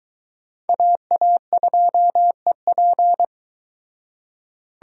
Morse code, heard as a keyed tone, about 23 wpm, 700 Hz. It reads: AA2EP